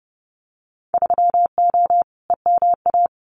Morse code, heard as a keyed tone, about 30 wpm, 700 Hz. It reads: 3O EMA
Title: Winter Field Day Exchanges Encore 30wpm